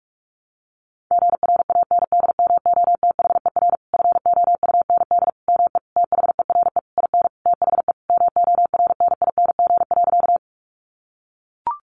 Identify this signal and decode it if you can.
Morse code, heard as a keyed tone, about 44 wpm, 700 Hz. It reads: GRANDMOTHER FOUND ME THERE IN THE MORNING.
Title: Sentences from Top 1200 Words 45wpm